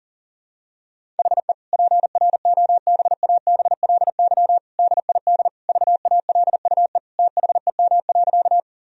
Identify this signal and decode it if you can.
Morse code, heard as a keyed tone, about 40 wpm, 700 Hz. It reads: HE PROBABLY DID VALUE THEM.
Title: Sentences from Top 1000 Words - Review Part 4 - 40wpm